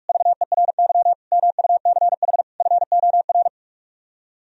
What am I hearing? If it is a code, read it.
Morse code, heard as a keyed tone, about 44 wpm, 700 Hz. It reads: VERY MUCH FOR